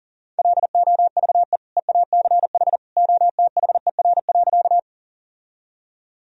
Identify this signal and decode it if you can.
Morse code, heard as a keyed tone, about 40 wpm, 700 Hz. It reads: LOVE EACH OTHER.
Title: Sentences from Top 1000 Words - Review Part 1 - 40wpm